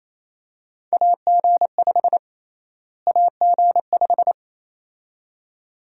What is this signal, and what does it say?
Morse code, heard as a keyed tone, about 28 wpm, 700 Hz. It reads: AG5 AG5
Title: USA Call Sign Prefixes - Repeated 2x - 28wpm